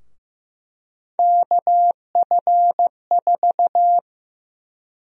Morse code, heard as a keyed tone, about 15 wpm, 700 Hz. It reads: KF4
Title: USA Call Sign Prefixes - Repeated 2x - 15wpm